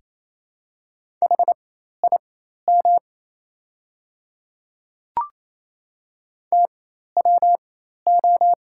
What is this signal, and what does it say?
Morse code, heard as a keyed tone, about 28 wpm, spaced as for 13 wpm, 690 Hz. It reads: HIM TWO
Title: 4x Character Spacing - Top 100 Words - 28wpm